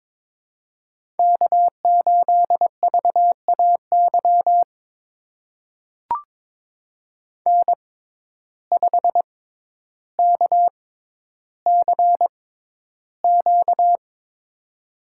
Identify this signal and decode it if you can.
Morse code, heard as a keyed tone, about 22 wpm, spaced as for 11 wpm, 700 Hz. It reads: K8VAY N5KCQ
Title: USA Call Signs - 5 Characters - 6x Character Spacing - 22wpm